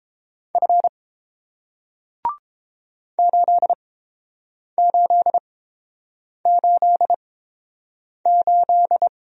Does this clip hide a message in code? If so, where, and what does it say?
Yes: Morse code from 0.6 to 9.1 s, heard as a keyed tone, about 32 wpm, spaced as for 8 wpm, 700 Hz. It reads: F 8888